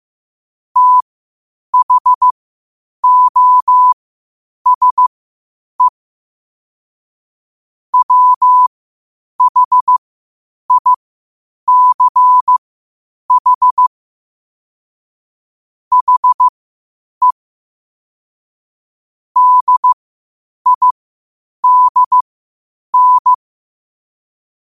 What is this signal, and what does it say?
Morse code, heard as a keyed tone, about 15 wpm, spaced as for 8 wpm, 1000 Hz. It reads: THOSE WHICH HE DIDN
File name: qotdfiles / 8wpm